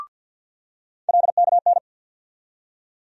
Morse code, heard as a keyed tone, about 47 wpm, 700 Hz. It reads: LKN
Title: USA Call Sign Postfixes - 3 Letters 50wpm